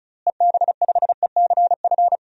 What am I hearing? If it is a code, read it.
Morse code, heard as a keyed tone, about 36 wpm, 700 Hz. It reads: EB5ECF
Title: DXCC-100 Call Signs - 6 Characters - Repeated 2x - 35wpm